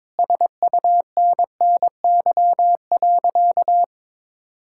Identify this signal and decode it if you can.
Morse code, heard as a keyed tone, about 22 wpm, 700 Hz. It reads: SUNNY.
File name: practice / 22wpm / Sentences with 3-Word Phrases - 22wpm